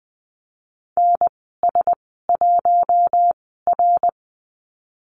Morse code, heard as a keyed tone, about 20 wpm, spaced as for 14 wpm, 700 Hz. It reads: NS1R
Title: 055 Lesson 10g - Letter r - Call Signs 20wpm